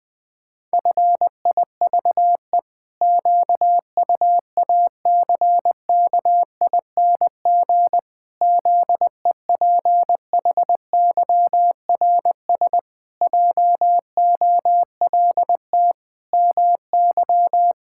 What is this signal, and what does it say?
Morse code, heard as a keyed tone, about 20 wpm, 700 Hz. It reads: FIVE QUACKING ZEPHYRS JOLT MY
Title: Pangram Challenge v2 - 20wpm